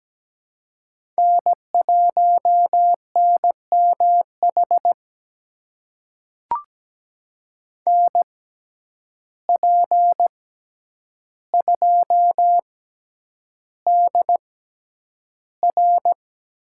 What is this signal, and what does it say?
Morse code, heard as a keyed tone, about 17 wpm, spaced as for 9 wpm, 700 Hz. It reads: N1NMH NP2DR
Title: USA Call Signs - 5 Characters - 6x Character Spacing - 17wpm